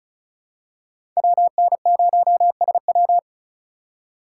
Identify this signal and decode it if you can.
Morse code, heard as a keyed tone, about 36 wpm, 700 Hz. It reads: WN0SW
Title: USA Call Signs - 5 Characters - 2x Character Spacing - 35wpm